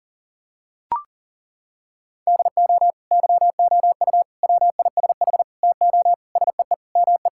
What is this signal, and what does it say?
Morse code, heard as a keyed tone, about 40 wpm, 700 Hz. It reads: DO YOU WISH TO SEE ME
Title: Sentences from Top 1600 Words 40wpm